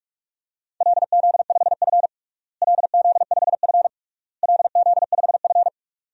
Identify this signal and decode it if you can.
Morse code, heard as a keyed tone, about 41 wpm, 700 Hz. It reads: LZ5F LZ5F LZ5F
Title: DXCC-100 Call Signs - 4 Characters - Repeated 3x - 45wpm